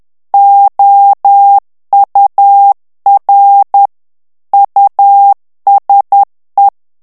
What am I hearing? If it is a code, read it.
Morse code, heard as a keyed tone, about 10 wpm, 800 Hz. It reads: OUR USE
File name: ANGLAIS texte clair 3 vitesse 10 mots minute